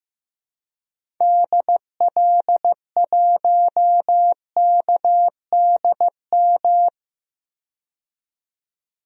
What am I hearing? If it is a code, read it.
Morse code, heard as a keyed tone, about 15 wpm, 700 Hz. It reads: DL1KDM
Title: DXCC-100 Call Signs - 6 Characters - 15wpm